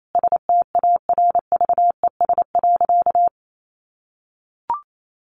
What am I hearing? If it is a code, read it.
Morse code, heard as a keyed tone, about 28 wpm, 700 Hz. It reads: STARVES.
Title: A Sound Of Thunder 28wpm